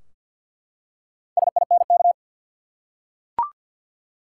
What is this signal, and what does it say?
Morse code, heard as a keyed tone, about 52 wpm, 700 Hz. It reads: SINK
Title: Four Letter Words - 50wpm